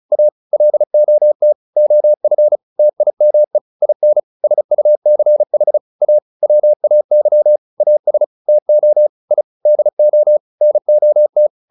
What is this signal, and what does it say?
Morse code, heard as a keyed tone, about 34 wpm, 600 Hz. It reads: A LOT OF TIME IN SUCH A WAY AS TO I DO NOT
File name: WD403 35